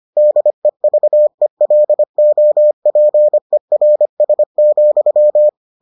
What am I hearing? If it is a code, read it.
Morse code, heard as a keyed tone, about 25 wpm, 600 Hz. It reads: DEVELOPERS,